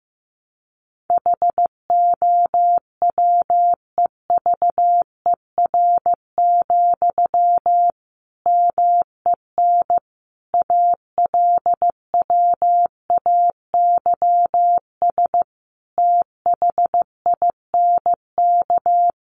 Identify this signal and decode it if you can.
Morse code, heard as a keyed tone, about 15 wpm, 700 Hz. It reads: HOWEVER, MEN ALWAYS THINK
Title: Sentences from Top 600 Words 15wpm